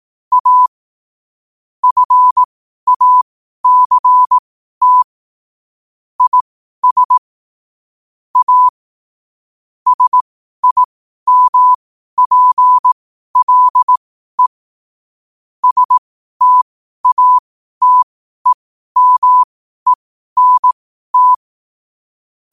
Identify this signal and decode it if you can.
Morse code, heard as a keyed tone, about 18 wpm, spaced as for 12 wpm, 1000 Hz. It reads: A FACT IS A SIMPLE STATEMENT
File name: qotdfiles / 12wpm